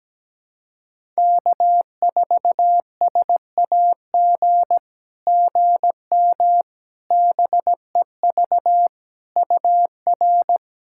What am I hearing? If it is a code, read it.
Morse code, heard as a keyed tone, about 17 wpm, 700 Hz. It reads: K4SAG GM BEV UR
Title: SOTA QSO Practice 17wpm